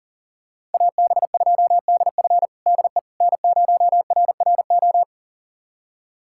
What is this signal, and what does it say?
Morse code, heard as a keyed tone, about 40 wpm, 700 Hz. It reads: AB2DF DE N0RRO
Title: USA Callsign DE Callsign 40wpm